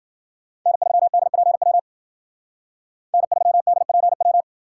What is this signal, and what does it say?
Morse code, heard as a keyed tone, about 52 wpm, 700 Hz. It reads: N3DPW N3DPW